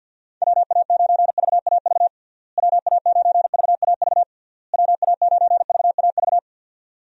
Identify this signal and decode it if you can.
Morse code, heard as a keyed tone, about 52 wpm, 700 Hz. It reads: WA9VAV WA9VAV WA9VAV
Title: USA Call Signs - 6 Characters - Repeated 3x - 50wpm